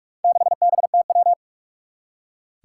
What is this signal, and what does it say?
Morse code, heard as a keyed tone, about 46 wpm, 700 Hz. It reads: 6BTW
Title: Four Challenging Letter-Number Combinations 45wpm